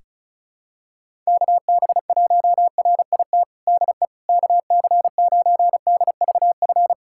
Morse code, heard as a keyed tone, about 36 wpm, 700 Hz. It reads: KB1RIT DE KC9DVF
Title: USA Callsign DE Callsign - 6 Characters - 35wpm